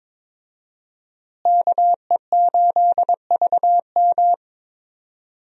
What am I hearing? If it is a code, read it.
Morse code, heard as a keyed tone, about 22 wpm, 700 Hz. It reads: KE8VM
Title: USA Call Signs - 5 Characters - Repeated 3x - 22wpm